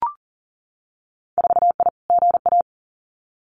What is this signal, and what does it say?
Morse code, heard as a keyed tone, about 39 wpm, 700 Hz. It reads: 4I GA
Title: Winter Field Day Exchanges Encore 40wpm